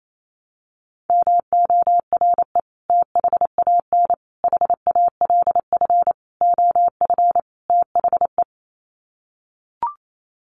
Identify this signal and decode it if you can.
Morse code, heard as a keyed tone, about 28 wpm, 700 Hz. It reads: MORE THAN HALF OF THE
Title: Sets of 5 Words 28wpm